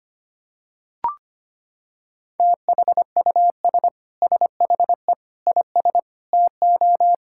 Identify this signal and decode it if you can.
Morse code, heard as a keyed tone, about 25 wpm, 700 Hz. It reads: THUS SHE IS TO
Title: Sentences from Top 1000 Words - Review Part 5 - 25wpm